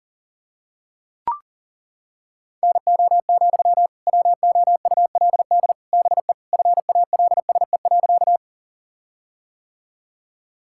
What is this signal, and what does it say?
Morse code, heard as a keyed tone, about 40 wpm, 700 Hz. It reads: NO, WOULD BE FALSE.
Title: Sentences from Top 1600 Words 40wpm